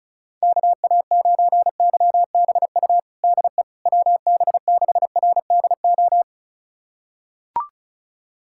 Morse code, heard as a keyed tone, about 36 wpm, 700 Hz. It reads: KA9YBU DE WB6RDO
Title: USA Callsign DE Callsign 35wpm